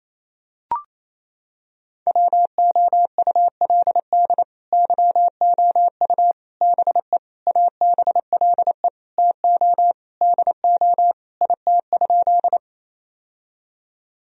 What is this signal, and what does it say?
Morse code, heard as a keyed tone, about 28 wpm, 700 Hz. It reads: WOULD YOU BE ABLE TO DO IT?
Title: Sentences from Top 1000 Words - Review Part 2 - 28wpm